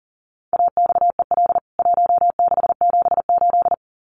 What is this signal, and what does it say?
Morse code, heard as a keyed tone, about 40 wpm, 700 Hz. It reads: AXEL 1678
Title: CWT Exchange 40wpm